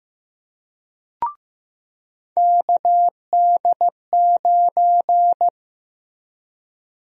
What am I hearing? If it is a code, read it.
Morse code, heard as a keyed tone, about 15 wpm, 700 Hz. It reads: KD9
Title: USA Call Sign Prefixes - Repeated 2x - 15wpm